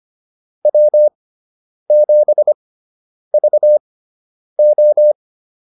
Morse code, heard as a keyed tone, about 25 wpm, spaced as for 9 wpm, 600 Hz. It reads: W7VO
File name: QSO 105 09